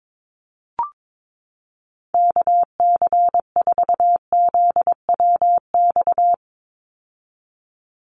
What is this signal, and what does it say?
Morse code, heard as a keyed tone, about 22 wpm, 700 Hz. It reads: KC4ZWX
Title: USA Call Signs 22wpm